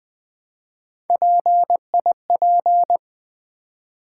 Morse code, heard as a keyed tone, about 20 wpm, 700 Hz. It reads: PIP
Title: Build A Word - 3 Letters At A Time - 8-Letter Words - 20wpm